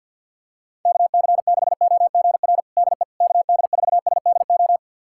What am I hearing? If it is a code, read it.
Morse code, heard as a keyed tone, about 52 wpm, 690 Hz. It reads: KC6OGR DE KD4SDO
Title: USA Callsign DE Callsign 50wpm